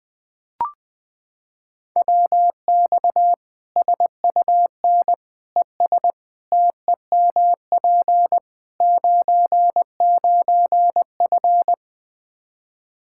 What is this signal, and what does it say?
Morse code, heard as a keyed tone, about 20 wpm, 700 Hz. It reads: WX SUN ES TEMP 99F
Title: Practice QSO - Exchange 2 - Info - 20wpm